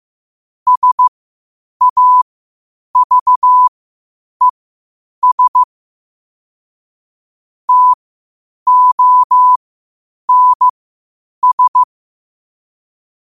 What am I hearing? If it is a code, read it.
Morse code, heard as a keyed tone, about 15 wpm, spaced as for 8 wpm, 1000 Hz. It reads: SAVES TONS
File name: qotdfiles / 8wpm